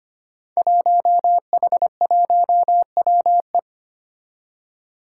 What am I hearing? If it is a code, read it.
Morse code, heard as a keyed tone, about 25 wpm, 700 Hz. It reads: 1H1WE
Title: Five Challenging Letter-Number Combinations 25wpm